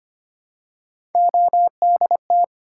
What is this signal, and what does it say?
Morse code, heard as a keyed tone, about 25 wpm, 700 Hz. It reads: ODT
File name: Three Letter-Number Combinations - 25wpm